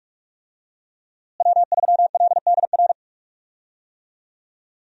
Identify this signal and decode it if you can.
Morse code, heard as a keyed tone, about 46 wpm, 700 Hz. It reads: W3LDR
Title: USA Call Signs - 5 Characters - Repeated 3x - 45wpm